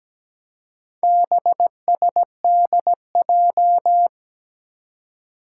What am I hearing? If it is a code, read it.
Morse code, heard as a keyed tone, about 17 wpm, 700 Hz. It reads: BSDJ
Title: Four Challenging Letter-Number Combinations 17wpm